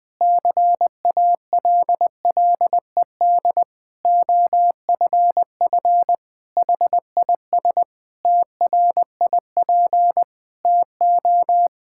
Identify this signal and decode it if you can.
Morse code, heard as a keyed tone, about 20 wpm, 700 Hz. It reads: CALLED OFF HIS TRIP TO
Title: Pangram Challenge v2 - 20wpm